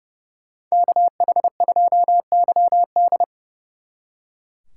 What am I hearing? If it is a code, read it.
Morse code, heard as a keyed tone, about 34 wpm, 700 Hz. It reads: KH2YD
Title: Speed Racing - USA 2x2 Call Signs 30wpm